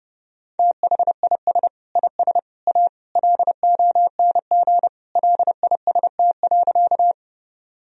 Morse code, heard as a keyed tone, about 30 wpm, 700 Hz. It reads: THIS IS A LONG LIST.
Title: Sentences from Top 1000 Words - Review Part 2 - 30wpm